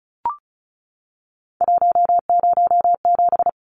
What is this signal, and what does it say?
Morse code, heard as a keyed tone, about 34 wpm, 700 Hz. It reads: 107
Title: Numbers - 3-Digit 35wpm